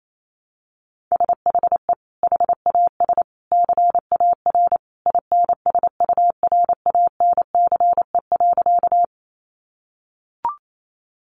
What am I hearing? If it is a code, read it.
Morse code, heard as a keyed tone, about 28 wpm, 700 Hz. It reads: SHE HAS CAR INSURANCE.
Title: Sentences from Top 1100 Words 28wpm